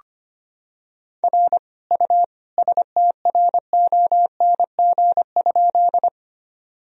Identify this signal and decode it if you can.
Morse code, heard as a keyed tone, about 25 wpm, 700 Hz. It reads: R U STRONG?
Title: Sentences from Top 500 Words and Abbreviations 25wpm